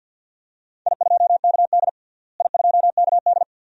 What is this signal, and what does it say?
Morse code, heard as a keyed tone, about 50 wpm, 700 Hz. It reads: I2KD I2KD
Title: DXCC-100 Call Signs - 4 Characters - Repeated 2x - 50wpm